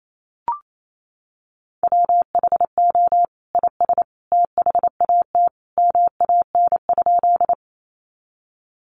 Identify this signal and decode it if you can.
Morse code, heard as a keyed tone, about 28 wpm, 700 Hz. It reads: WHO IS THAT MAN?
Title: Sentences from Top 200 Words 28wpm